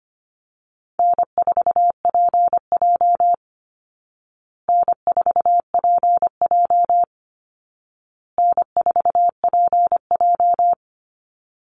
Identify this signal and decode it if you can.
Morse code, heard as a keyed tone, about 25 wpm, 700 Hz. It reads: N4PJ N4PJ N4PJ
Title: Easy Call Signs Repeated 3x - 25wpm